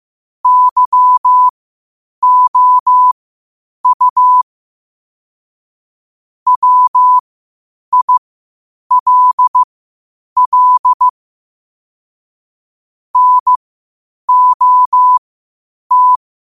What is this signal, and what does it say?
Morse code, heard as a keyed tone, about 15 wpm, spaced as for 8 wpm, 1000 Hz. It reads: YOU WILL NOT
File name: qotdfiles / 8wpm